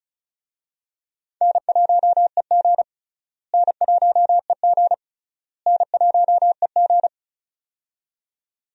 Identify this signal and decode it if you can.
Morse code, heard as a keyed tone, about 34 wpm, 700 Hz. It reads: N1EG N1EG N1EG
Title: USA Call Signs - 4 Characters - Repeated 3x - 35wpm